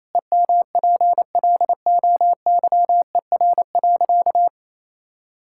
Morse code, heard as a keyed tone, about 27 wpm, 700 Hz. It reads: EMPLOYER.
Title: Sentences from Top 1600 Words 28wpm